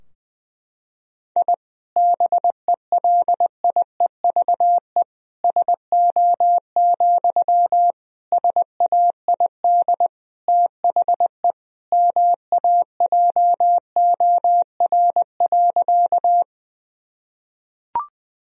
Morse code, heard as a keyed tone, about 21 wpm, 700 Hz. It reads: I BELIEVE SO, SAID THE MAJOR.
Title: Sentences from Top 1000 Words - Review Part 4 - 20wpm